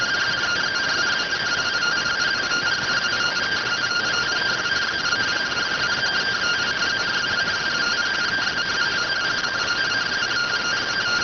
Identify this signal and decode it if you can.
Two-tone teleprinter signal, about 170 Hz shift.